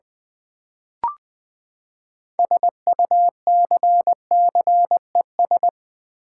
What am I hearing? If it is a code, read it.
Morse code, heard as a keyed tone, about 20 wpm, 700 Hz. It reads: SUCCES
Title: Build A Word - 6 Letters At A Time - 12-Letter Words - 20wpm